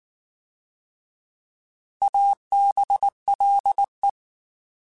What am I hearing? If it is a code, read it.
Morse code, heard as a keyed tone, about 19 wpm, 800 Hz. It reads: ABLE
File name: Alphabetical / Alphabetical mp3s / As @18